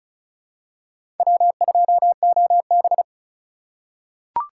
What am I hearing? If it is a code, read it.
Morse code, heard as a keyed tone, about 36 wpm, 700 Hz. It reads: W2OB
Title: USA Call Signs - 4 Characters - 4x Character Spacing - 35wpm